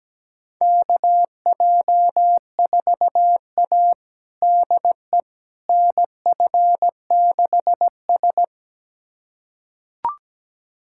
Morse code, heard as a keyed tone, about 17 wpm, 700 Hz. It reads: KJ4A DE NF6S
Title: USA Callsign DE Callsign - 4 Characters - 17wpm